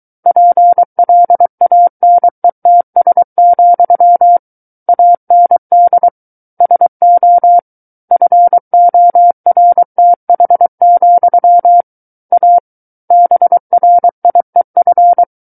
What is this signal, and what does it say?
Morse code, heard as a keyed tone, about 23 wpm, 700 Hz. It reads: PLANETS, AND SO FORTH, A BRIEF
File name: Book One / 14-Chapter 14 - 23 WPM